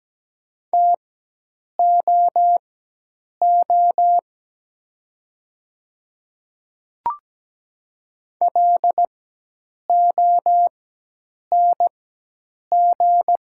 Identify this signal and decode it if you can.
Morse code, heard as a keyed tone, about 17 wpm, spaced as for 8 wpm, 700 Hz. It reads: TOO LONG
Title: 4x Character Spacing - Top 200 Words - 17wpm